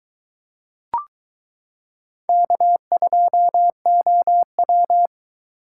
Morse code, heard as a keyed tone, about 22 wpm, 700 Hz. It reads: K2OW